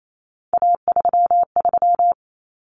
Morse code, heard as a keyed tone, about 28 wpm, 700 Hz. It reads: A33
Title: Common Cut Numbers - 3-Digit 28wpm